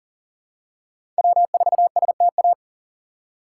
Morse code, heard as a keyed tone, about 37 wpm, 700 Hz. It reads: W4STA
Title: USA Call Signs - 5 Characters - 6x Character Spacing - 40wpm